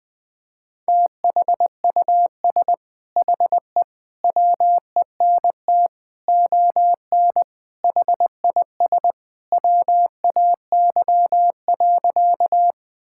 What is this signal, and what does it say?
Morse code, heard as a keyed tone, about 20 wpm, 700 Hz. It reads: THUS HE WENT ON HIS WAY.
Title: Sentences from Top 1000 Words - Review Part 5 - 20wpm